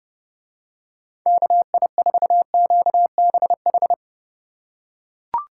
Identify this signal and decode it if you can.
Morse code, heard as a keyed tone, about 29 wpm, 700 Hz. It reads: KI4QBH